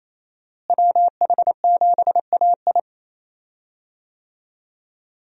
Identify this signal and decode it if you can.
Morse code, heard as a keyed tone, about 28 wpm, 700 Hz. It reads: WH7AI
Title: USA Call Signs - 5 Characters - Repeated 2x - 28wpm